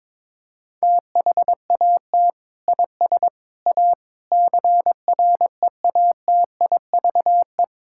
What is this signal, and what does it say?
Morse code, heard as a keyed tone, about 22 wpm, 700 Hz. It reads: THAT IS A CREATIVE
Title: Sentences from Top 1000 Words - Review Part 5 - 22wpm